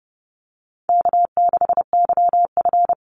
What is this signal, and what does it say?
Morse code, heard as a keyed tone, about 30 wpm, 700 Hz. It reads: K6YF
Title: Easy Call Signs Repeated 3x - 30wpm